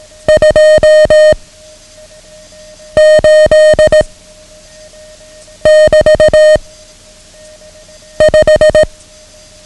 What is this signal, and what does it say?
Morse code, heard as a keyed tone, about 18 wpm, 620 Hz. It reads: 2 8 <BT> 5
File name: ham radio / CodeQuick / Code Quick 09